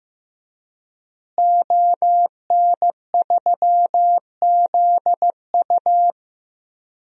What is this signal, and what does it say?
Morse code, heard as a keyed tone, about 15 wpm, 700 Hz. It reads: ON3ZU
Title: DXCC-100 Call Signs - 5 Characters - Repeated 3x - 15wpm